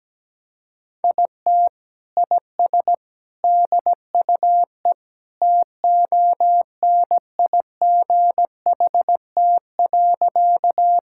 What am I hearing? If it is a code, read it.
Morse code, heard as a keyed tone, about 17 wpm, 700 Hz. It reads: IT IS DUE TONIGHT.
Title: Sentences from Top 1000 Words - Review Part 6 - 17wpm